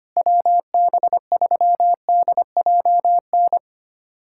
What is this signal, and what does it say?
Morse code, heard as a keyed tone, about 25 wpm, 700 Hz. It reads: WB3DJN
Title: USA Callsign DE Callsign 25wpm